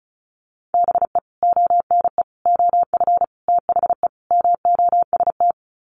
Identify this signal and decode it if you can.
Morse code, heard as a keyed tone, about 34 wpm, 700 Hz. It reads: BE ONE OF THE MOST
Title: Sets of 5 Words Encore 35wpm